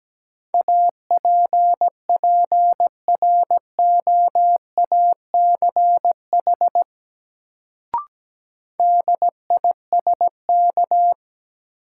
Morse code, heard as a keyed tone, about 17 wpm, 700 Hz. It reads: APPROACH DISK
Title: Sentences from Top 1000 Words - Review Part 6 - New Word Drill - 17wpm